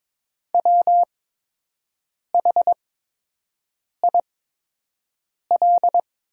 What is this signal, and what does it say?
Morse code, heard as a keyed tone, about 22 wpm, spaced as for 6 wpm, 700 Hz. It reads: WHIL
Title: 8x Character Spacing - Top 500 Words - 22wpm